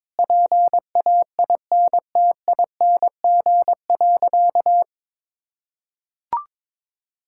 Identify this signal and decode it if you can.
Morse code, heard as a keyed tone, about 22 wpm, 700 Hz. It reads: PAINTING.
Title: Sentences from Top 1000 Words - Review Part 6 - 22wpm